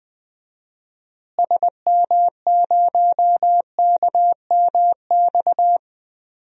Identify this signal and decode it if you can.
Morse code, heard as a keyed tone, about 20 wpm, 700 Hz. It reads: SM0KMX
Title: DXCC-100 Call Signs - 6 Characters - Repeated 2x - 20wpm